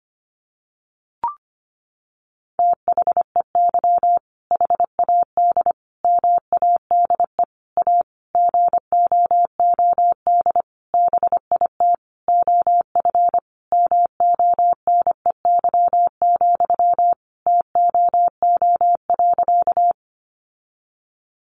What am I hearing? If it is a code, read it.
Morse code, heard as a keyed tone, about 25 wpm, 700 Hz. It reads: THEY HAD MADE A GOOD BIT OF MONEY, TOO.
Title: Sentences from Top 900 Words 25wpm